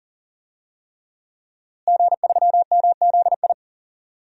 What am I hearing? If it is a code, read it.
Morse code, heard as a keyed tone, about 37 wpm, 700 Hz. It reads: G3MZI
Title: DXCC-100 Call Signs - 5 Characters - Repeated 3x - 40wpm